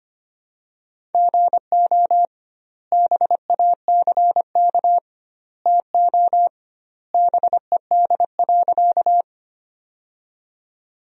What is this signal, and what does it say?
Morse code, heard as a keyed tone, about 25 wpm, 700 Hz. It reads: GO BACK TO BED.
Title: 2x Word-Spacing - Sentences from Top 500 Words - 25wpm